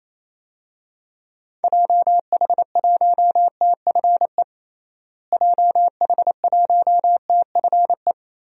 Morse code, heard as a keyed tone, about 28 wpm, 700 Hz. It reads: JH1TFE JH1TFE
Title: DXCC-100 Call Signs - 6 Characters - Repeated 2x - 28wpm